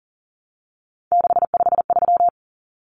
Morse code, heard as a keyed tone, about 40 wpm, 700 Hz. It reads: 653
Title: Numbers - 3-Digit 40wpm